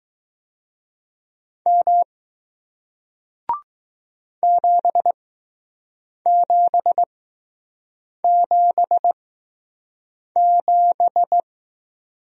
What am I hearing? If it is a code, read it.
Morse code, heard as a keyed tone, about 20 wpm, spaced as for 6 wpm, 700 Hz. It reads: M 7777